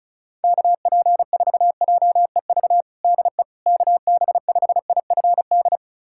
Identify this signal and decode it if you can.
Morse code, heard as a keyed tone, about 35 wpm, 700 Hz. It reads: KP4JEV DE KB5IFD
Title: USA Callsign DE Callsign 35wpm